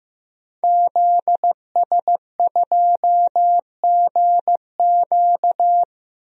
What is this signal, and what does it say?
Morse code, heard as a keyed tone, about 15 wpm, 700 Hz. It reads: ZS2GQ
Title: DXCC-100 Call Signs - 5 Characters - 15wpm